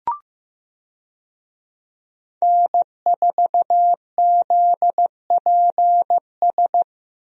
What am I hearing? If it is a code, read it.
Morse code, heard as a keyed tone, about 15 wpm, 700 Hz. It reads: N4ZPS
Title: USA Call Signs - 5 Characters - Repeated 3x - 15wpm